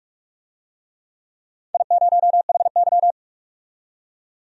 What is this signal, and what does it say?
Morse code, heard as a keyed tone, about 46 wpm, 700 Hz. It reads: I0HY
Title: DXCC-100 Call Signs - 4 Characters - 45wpm